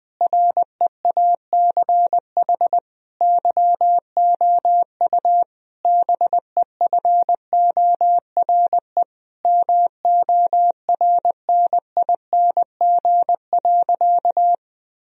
Morse code, heard as a keyed tone, about 20 wpm, 700 Hz. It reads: REACH YOU BEFORE MORNING.
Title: Sentences from Top 1000 Words - Review Part 2 - 20wpm